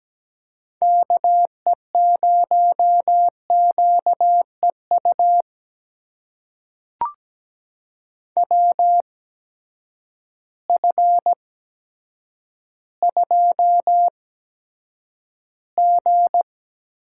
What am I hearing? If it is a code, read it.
Morse code, heard as a keyed tone, about 17 wpm, spaced as for 9 wpm, 700 Hz. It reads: KE0QEU WF2G
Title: USA Call Signs - 8x Character Spacing - 17wpm